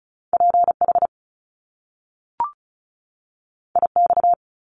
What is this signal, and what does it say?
Morse code, heard as a keyed tone, about 36 wpm, 700 Hz. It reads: PH IX
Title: Top Two Letter Combinations in Words 35wpm